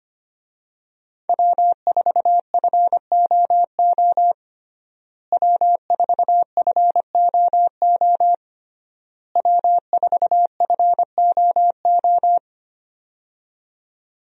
Morse code, heard as a keyed tone, about 25 wpm, 700 Hz. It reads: W4FOO W4FOO W4FOO
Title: USA Call Signs - 5 Characters - Repeated 3x - 25wpm